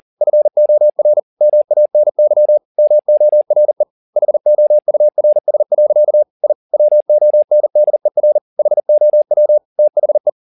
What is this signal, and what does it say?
Morse code, heard as a keyed tone, about 40 wpm, 600 Hz. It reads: FOR MANY MORE HOURS. I WONDER HOW THE